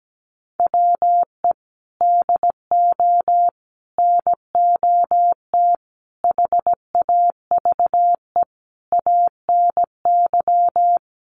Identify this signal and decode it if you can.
Morse code, heard as a keyed tone, about 17 wpm, 700 Hz. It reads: WE DO NOT HAVE ANY
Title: Sets of 5 Words 17wpm